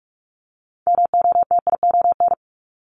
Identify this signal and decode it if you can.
Morse code, heard as a keyed tone, about 46 wpm, 700 Hz. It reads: MOTION
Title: Multi-letter Phonemes 45wpm